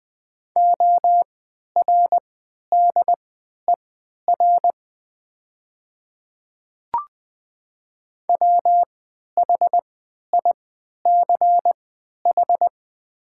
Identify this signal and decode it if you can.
Morse code, heard as a keyed tone, about 21 wpm, spaced as for 11 wpm, 690 Hz. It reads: ORDER WHICH
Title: 3x Character Spacing - Top 500 Words - 20wpm